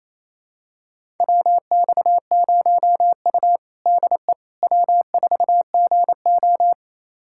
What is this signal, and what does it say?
Morse code, heard as a keyed tone, about 28 wpm, 700 Hz. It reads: WX0U DE W4GO
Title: USA Callsign DE Callsign - 4 Characters - 28wpm